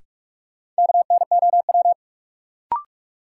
Morse code, heard as a keyed tone, about 47 wpm, 700 Hz. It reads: KNOW
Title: Sets of 2 Words - New Word Drill - 45wpm